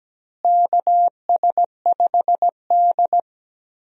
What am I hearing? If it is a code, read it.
Morse code, heard as a keyed tone, about 17 wpm, 700 Hz. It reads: KS5D
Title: USA Call Signs - 4 Characters - Repeated 3x - 17wpm